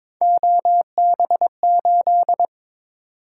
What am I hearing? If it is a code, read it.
Morse code, heard as a keyed tone, about 22 wpm, 700 Hz. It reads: OB8
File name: practice / 22wpm / Three Letter-Number Combinations - 22wpm